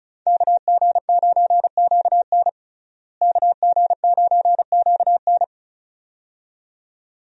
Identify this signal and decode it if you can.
Morse code, heard as a keyed tone, about 34 wpm, 700 Hz. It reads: KG9QN KG9QN
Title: USA Call Signs - 5 Characters - Repeated 2x - 35wpm